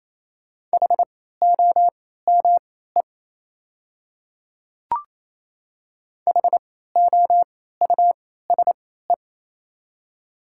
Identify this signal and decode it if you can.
Morse code, heard as a keyed tone, about 28 wpm, spaced as for 16 wpm, 700 Hz. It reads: HOME HOUSE
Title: 3x Character Spacing - Top 200 Words - 28wpm